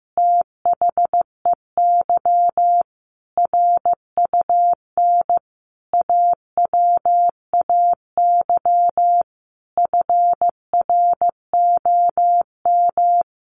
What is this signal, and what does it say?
Morse code, heard as a keyed tone, about 15 wpm, 700 Hz. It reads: THEY RUN AWAY FROM
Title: Easy Phrases 15wpm